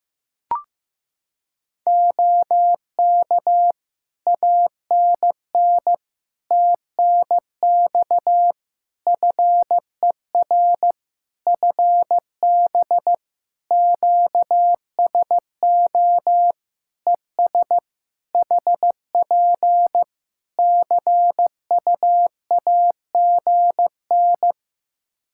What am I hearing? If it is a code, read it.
Morse code, heard as a keyed tone, about 15 wpm, 700 Hz. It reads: OK ANN TNX FER FB QSO ES HP CUAGN
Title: Practice QSO - Exchange 4 - Ending - 15wpm